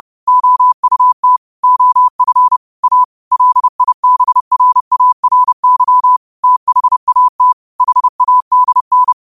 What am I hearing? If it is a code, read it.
Morse code, heard as a keyed tone, about 30 wpm, 1000 Hz. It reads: OUT OF A LIBRARY THAT HADN
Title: 30 WPM morse code quotes for Thu, 29 Feb 2024